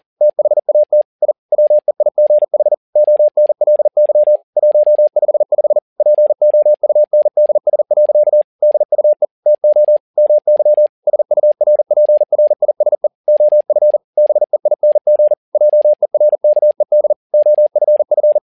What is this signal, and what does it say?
Morse code, heard as a keyed tone, about 40 wpm, 600 Hz. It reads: THAT I WEIGH ONLY 155 POUNDS. DUE TO MY SURPRISE OF BEING JERKED OFF
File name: learning center / tower trouble / tower touble 40wpm